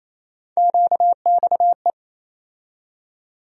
Morse code, heard as a keyed tone, about 27 wpm, 700 Hz. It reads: QXE
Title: USA Call Sign Postfixes - 3 Letters 28wpm